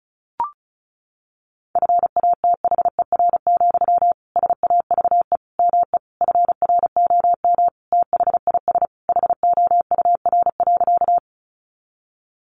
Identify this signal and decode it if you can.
Morse code, heard as a keyed tone, about 34 wpm, 700 Hz. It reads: FATHER, SAVE ME FROM THIS HOUR.
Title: Sentences from Top 1100 Words 35wpm